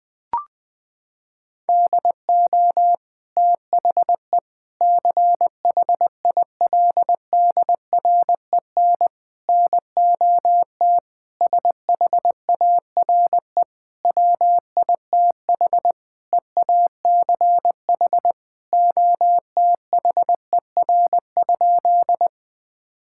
Morse code, heard as a keyed tone, about 20 wpm, 700 Hz. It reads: DO THE CHILDREN NOT SHARE WITH EACH OTHER?
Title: Sentences from Top 1000 Words - Review Part 6 - 20wpm